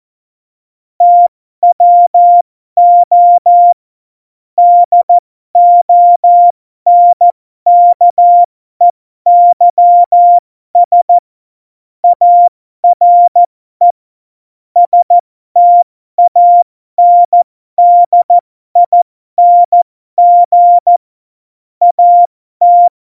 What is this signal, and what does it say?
Morse code, heard as a keyed tone, about 14 wpm, spaced as for 12 wpm, 700 Hz. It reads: TWO DONKEYS ARE STANDING AT
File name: Morsekurs / Witze